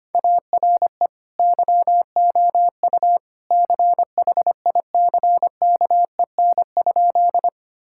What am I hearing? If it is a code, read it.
Morse code, heard as a keyed tone, about 25 wpm, 700 Hz. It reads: ARE YOU CHICKEN?
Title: Sentences from Top 1000 Words - Review Part 5 - 25wpm